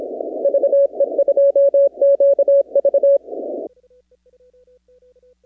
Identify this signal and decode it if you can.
Morse code, heard as a keyed tone, about 27 wpm, 560 Hz. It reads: VE2QV
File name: contests / 2019-cw-wpx-kt5j-k5pi- -2330 0-38